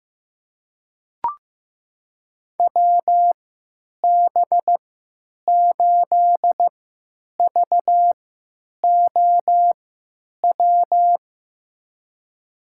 Morse code, heard as a keyed tone, about 15 wpm, spaced as for 9 wpm, 700 Hz. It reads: WB8VOW
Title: USA Call Signs - 6 Characters - 3x Character Spacing - 15wpm